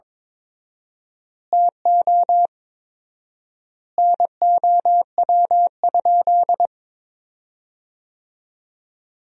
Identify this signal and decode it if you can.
Morse code, heard as a keyed tone, about 22 wpm, 700 Hz. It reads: TO NOW?
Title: 4x Word-Spacing - Sentences from Top 100 Words - 22wpm